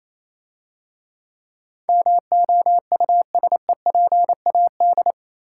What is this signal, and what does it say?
Morse code, heard as a keyed tone, about 27 wpm, 700 Hz. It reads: MOUSEPAD
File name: Easy Words - Speed-Racing with Hints - 22wpm